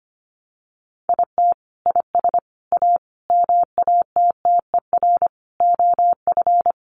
Morse code, heard as a keyed tone, about 25 wpm, 700 Hz. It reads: IT IS A MATTER OF
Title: Sentences from Top 700 Words 25wpm